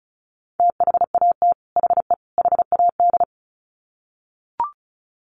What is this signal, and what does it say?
Morse code, heard as a keyed tone, about 34 wpm, 700 Hz. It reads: THAT HE HAD
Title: Sets of 3 Words 35wpm